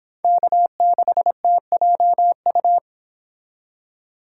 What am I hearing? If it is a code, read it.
Morse code, heard as a keyed tone, about 29 wpm, 700 Hz. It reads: K6TJU